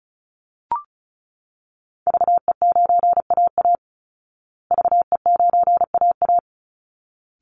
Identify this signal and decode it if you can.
Morse code, heard as a keyed tone, about 34 wpm, 700 Hz. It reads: VE9AA VE9AA
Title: Call Signs Repeated 3x - 35wpm